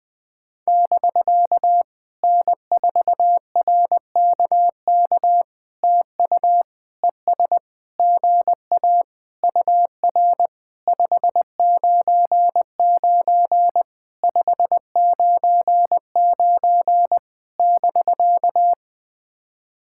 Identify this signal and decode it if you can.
Morse code, heard as a keyed tone, about 20 wpm, 700 Hz. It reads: <BK> N4RKK TU ES GA UR 599 599 <BK>
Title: POTA QSO Practice - 20wpm